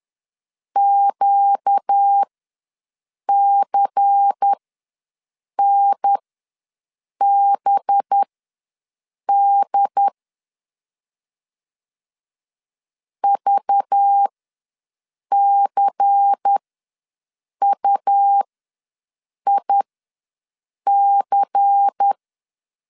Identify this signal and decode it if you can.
Morse code, heard as a keyed tone, about 10 wpm, spaced as for 6 wpm, 800 Hz. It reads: QCNBD VCUIC
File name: LEZIONI / 5 LBVC / 5 LBVC ws 10wpm 3733